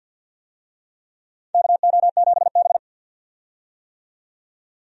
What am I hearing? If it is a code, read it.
Morse code, heard as a keyed tone, about 47 wpm, 700 Hz. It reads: KO7B
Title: USA Call Signs - 4 Characters - 50wpm